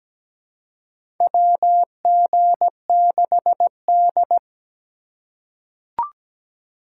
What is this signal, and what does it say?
Morse code, heard as a keyed tone, about 17 wpm, 700 Hz. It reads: WG6D
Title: USA Call Signs - 2x Character Spacing - 17wpm